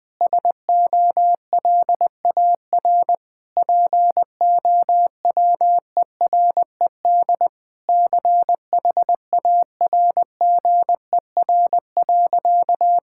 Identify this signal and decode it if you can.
Morse code, heard as a keyed tone, about 20 wpm, 700 Hz. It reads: SOLAR POWERED CHARGER.
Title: Sentences with 6-Word Phrases - Spoken Beginning - 20wpm